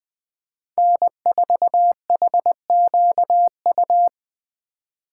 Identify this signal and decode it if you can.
Morse code, heard as a keyed tone, about 20 wpm, 700 Hz. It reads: N4HQU
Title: USA Call Signs - 5 Characters - 2x Character Spacing - 20wpm